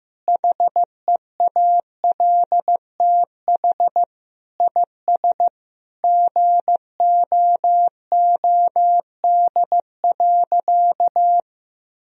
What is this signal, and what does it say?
Morse code, heard as a keyed tone, about 15 wpm, 700 Hz. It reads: HEALTH IS GOOD.
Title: Sentences from Top 1000 Words - Review Part 3 - 15wpm